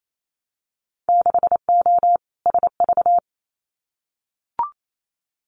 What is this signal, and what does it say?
Morse code, heard as a keyed tone, about 28 wpm, 700 Hz. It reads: 6O SV
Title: Winter Field Day Exchanges Encore 28wpm